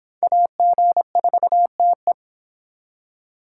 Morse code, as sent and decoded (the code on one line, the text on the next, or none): .- --. ....- - .
AG4TE